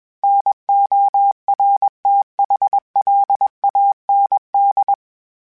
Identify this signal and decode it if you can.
Morse code, heard as a keyed tone, about 21 wpm, 800 Hz. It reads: NORTHLAND
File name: fromSMs / FromSMs mp3s / Cities@20